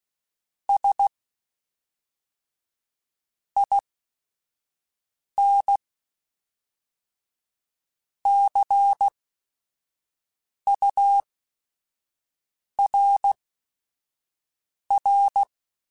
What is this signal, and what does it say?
Morse code, heard as a keyed tone, about 16 wpm, spaced as for 5 wpm, 800 Hz. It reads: S IN CURR
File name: Technical Subjects / Technical Subjects-mp3s / XL @5